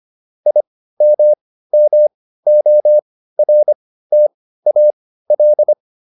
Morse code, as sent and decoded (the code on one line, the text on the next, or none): .. -- -- --- .-. - .- .-..
IMMORTAL